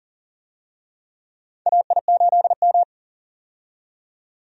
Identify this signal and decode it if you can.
Morse code, heard as a keyed tone, about 40 wpm, 700 Hz. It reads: AI8M
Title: USA Call Signs - 4 Characters - 40wpm